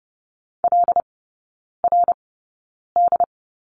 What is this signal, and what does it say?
Morse code, heard as a keyed tone, about 30 wpm, spaced as for 9 wpm, 700 Hz. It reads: LRD
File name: 081 Lesson 14e - Number 2 - Three Characters 30wpm